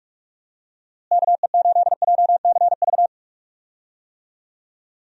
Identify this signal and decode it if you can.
Morse code, heard as a keyed tone, about 42 wpm, 700 Hz. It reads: KE8JCV
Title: USA Call Signs - 6 Characters - Repeated 2x - 45wpm